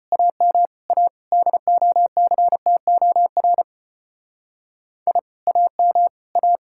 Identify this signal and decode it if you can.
Morse code, heard as a keyed tone, about 29 wpm, 700 Hz. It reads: AM A DOCTOR I AM A
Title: Speed-Racing Common QSO Phrases - 25wpm